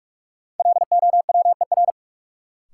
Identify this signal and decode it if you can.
Morse code, heard as a keyed tone, about 41 wpm, 700 Hz. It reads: LOWER
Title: Sentences from Top 700 Words - New Word Drill - 45wpm